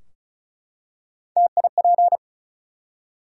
Morse code, heard as a keyed tone, about 36 wpm, 700 Hz. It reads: TIP
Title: Three Letter Words - 35wpm